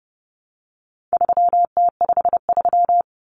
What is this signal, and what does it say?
Morse code, heard as a keyed tone, about 30 wpm, 700 Hz. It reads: 3T53
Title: Common Cut Numbers - 4-Digit 30wpm